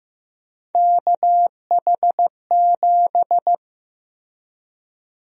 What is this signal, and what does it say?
Morse code, heard as a keyed tone, about 15 wpm, 700 Hz. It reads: KH7
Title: USA Call Sign Prefixes - 15wpm